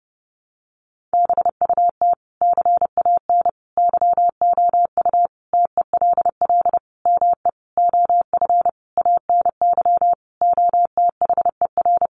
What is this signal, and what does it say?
Morse code, heard as a keyed tone, about 29 wpm, 700 Hz. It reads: BUT CAN YOU TELL ME OF ANY OTHER
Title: Sentences from Top 1400 Words 30wpm